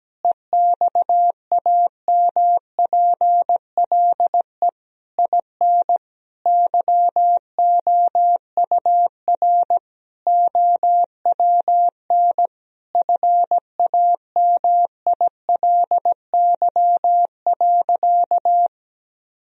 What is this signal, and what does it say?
Morse code, heard as a keyed tone, about 17 wpm, 700 Hz. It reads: EXAMPLE IN YOUR OWN FAMILY.
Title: Sentences from Top 1000 Words - Review Part 1 - 17wpm